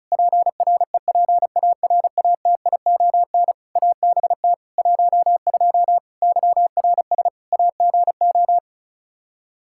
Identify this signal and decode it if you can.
Morse code, heard as a keyed tone, about 36 wpm, 700 Hz. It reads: PREPARATION ABT 12 YRS AGO
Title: K1USN SST QSO Practice 35wpm